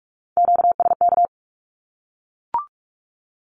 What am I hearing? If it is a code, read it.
Morse code, heard as a keyed tone, about 44 wpm, 700 Hz. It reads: QSK
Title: Q-Codes Repeated 3x 45wpm